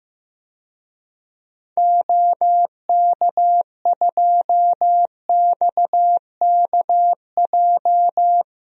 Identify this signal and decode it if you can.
Morse code, heard as a keyed tone, about 15 wpm, 700 Hz. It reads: OK2XKJ
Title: DXCC-100 Call Signs - 6 Characters - 15wpm